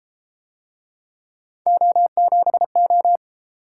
Morse code, heard as a keyed tone, about 31 wpm, 700 Hz. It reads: O7O